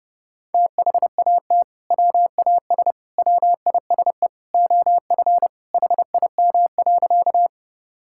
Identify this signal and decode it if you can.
Morse code, heard as a keyed tone, about 30 wpm, 700 Hz. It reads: THAT WAS WISE OF HIM.
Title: Sentences from Top 1600 Words 30wpm